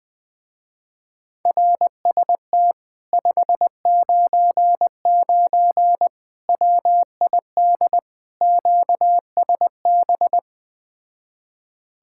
Morse code, heard as a keyed tone, about 20 wpm, 700 Hz. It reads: RST 599 WID QSB